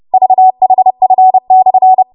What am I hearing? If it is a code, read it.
Morse code, heard as a keyed tone, about 30 wpm, 750 Hz. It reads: VHF/
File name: Morse / 30WPM